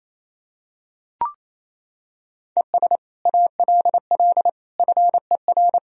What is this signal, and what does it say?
Morse code, heard as a keyed tone, about 28 wpm, 700 Hz. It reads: ES ALL FER
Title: Sentences from Top 100 Words and Abbreviations 28wpm